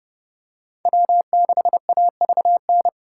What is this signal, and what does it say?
Morse code, heard as a keyed tone, about 29 wpm, 700 Hz. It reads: W6AVN